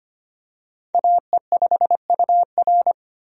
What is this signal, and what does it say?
Morse code, heard as a keyed tone, about 25 wpm, 690 Hz. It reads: AE5UR
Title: USA Call Signs - 5 Characters - 25wpm